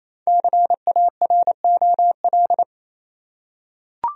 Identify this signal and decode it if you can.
Morse code, heard as a keyed tone, about 28 wpm, 700 Hz. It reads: CAROL
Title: USA Common Names - Max 5 Letters - 28wpm